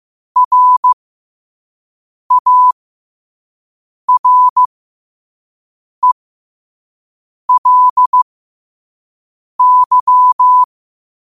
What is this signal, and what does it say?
Morse code, heard as a keyed tone, about 15 wpm, spaced as for 5 wpm, 1000 Hz. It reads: RARELY
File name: qotdfiles / 5wpm